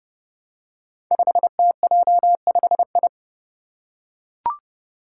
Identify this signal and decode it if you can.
Morse code, heard as a keyed tone, about 31 wpm, 700 Hz. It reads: 5TJ5I
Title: Five Challenging Letter-Number Combinations 30wpm